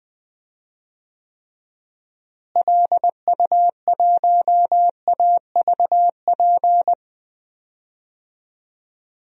Morse code, heard as a keyed tone, about 20 wpm, 700 Hz. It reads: LU1AVP
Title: DXCC-100 Call Signs - 6 Characters - Repeated 3x - 20wpm